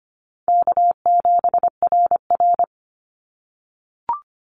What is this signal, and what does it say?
Morse code, heard as a keyed tone, about 25 wpm, 700 Hz. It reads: K7RR
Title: Call Signs - 25wpm